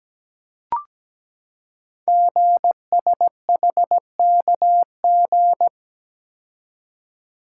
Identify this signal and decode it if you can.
Morse code, heard as a keyed tone, about 17 wpm, 700 Hz. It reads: GSHKG
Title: Code Groups - 5 Letters - 17wpm